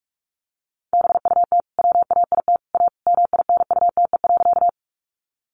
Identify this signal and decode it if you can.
Morse code, heard as a keyed tone, about 44 wpm, 700 Hz. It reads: BUT WAIT A MINUTE.
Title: Sentences from Top 500 Words 45wpm